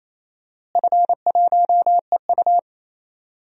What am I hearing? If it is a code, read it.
Morse code, heard as a keyed tone, about 28 wpm, 700 Hz. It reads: F1EU
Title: DXCC-100 Call Signs - 4 Characters - Repeated 2x - 28wpm